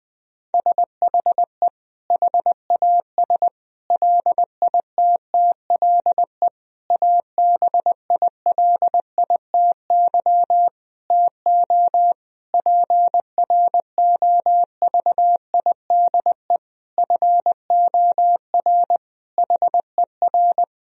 Morse code, heard as a keyed tone, about 21 wpm, 700 Hz. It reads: SHE HAS LITTLE ABILITY TO PROVIDE FOR HER
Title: Sentences from Top 1000 Words - Review Part 3 - 20wpm